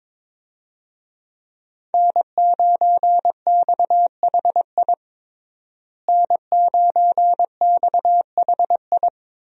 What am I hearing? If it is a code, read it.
Morse code, heard as a keyed tone, about 22 wpm, 700 Hz. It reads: N9XHI N9XHI
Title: USA Call Signs - 5 Characters - Repeated 3x - 22wpm